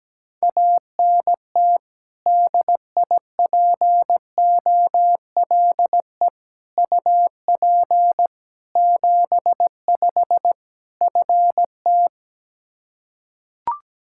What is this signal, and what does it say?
Morse code, heard as a keyed tone, about 17 wpm, 700 Hz. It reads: ANT DIPOLE UP 75 FT
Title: K1USN SST QSO Practice 17wpm